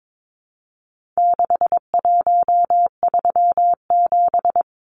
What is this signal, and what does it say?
Morse code, heard as a keyed tone, about 22 wpm, 700 Hz. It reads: 6137
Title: Numbers - 4-Digit 22wpm